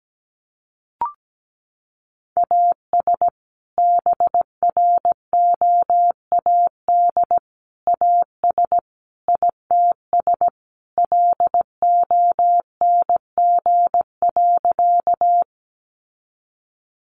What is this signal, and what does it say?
Morse code, heard as a keyed tone, about 17 wpm, 700 Hz. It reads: AS BROAD AS ITS LONG.
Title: Sentences from Top 1000 Words 17wpm